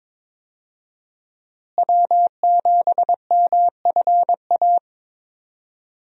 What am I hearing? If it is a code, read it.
Morse code, heard as a keyed tone, about 22 wpm, 700 Hz. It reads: W7MFA
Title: USA Call Signs - 5 Characters - Repeated 2x - 22wpm